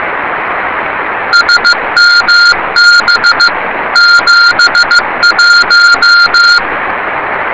Morse code, heard as a keyed tone, about 15 wpm, 1410 Hz. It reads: SMB 71
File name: rs30 cw